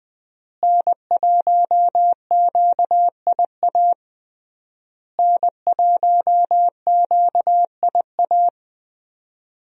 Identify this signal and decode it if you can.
Morse code, heard as a keyed tone, about 20 wpm, 700 Hz. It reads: N1QIA N1QIA
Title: USA Call Signs - 5 Characters - Repeated 3x - 20wpm